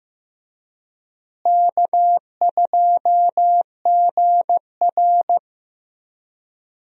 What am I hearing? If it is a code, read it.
Morse code, heard as a keyed tone, about 15 wpm, 700 Hz. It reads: K2GR
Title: USA Call Signs - 4 Characters - Repeated 2x - 15wpm